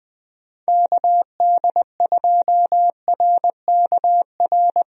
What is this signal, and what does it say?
Morse code, heard as a keyed tone, about 20 wpm, 700 Hz. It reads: KD2RKR
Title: USA Call Signs - 6 Characters - Repeated 2x - 20wpm